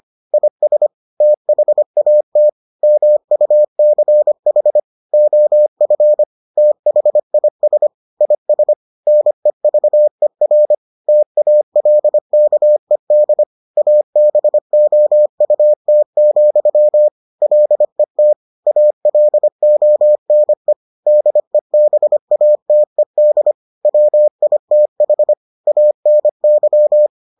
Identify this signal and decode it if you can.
Morse code, heard as a keyed tone, about 25 wpm, 600 Hz. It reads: IS THAT MUCH OF THIS IS NEVER TALKED ABOUT, LET ALONE DEBATED WITH ANY